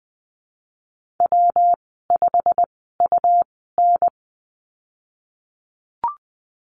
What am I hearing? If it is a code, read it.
Morse code, heard as a keyed tone, about 20 wpm, spaced as for 15 wpm, 700 Hz. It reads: W5UN
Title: 128 Lesson 21g - Number 6 - Call Signs 20wpm